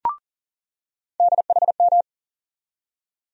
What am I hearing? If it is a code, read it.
Morse code, heard as a keyed tone, about 38 wpm, 700 Hz. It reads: DHM
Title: USA Call Sign Postfixes - 3 Letters 40wpm